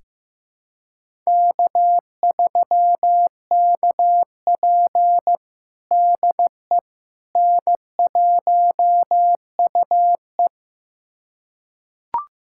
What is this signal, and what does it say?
Morse code, heard as a keyed tone, about 15 wpm, 700 Hz. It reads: K3KP DE N1UE
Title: USA Callsign DE Callsign - 4 Characters - 15wpm